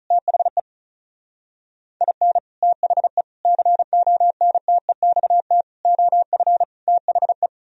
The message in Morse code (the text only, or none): THE IN THE CONTEXT OF THE